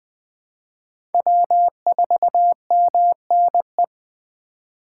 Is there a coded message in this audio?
Yes: Morse code from 1.1 to 3.8 s, heard as a keyed tone, about 21 wpm, 700 Hz. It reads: W4MNE